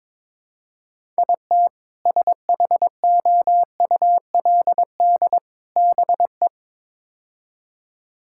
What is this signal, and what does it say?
Morse code, heard as a keyed tone, about 22 wpm, 700 Hz. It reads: IT SHOULD BE
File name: practice / 22wpm / Build A Phrase - 5-Word Phrases - 22wpm